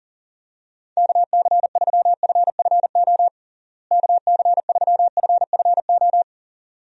Morse code, heard as a keyed tone, about 37 wpm, 700 Hz. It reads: KC3FFO KC3FFO
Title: USA Call Signs - 6 Characters - Repeated 3x - 40wpm